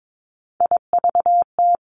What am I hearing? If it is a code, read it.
Morse code, heard as a keyed tone, about 22 wpm, 700 Hz. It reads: IVT
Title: Three Challenging Letter-Number Combinations 22wpm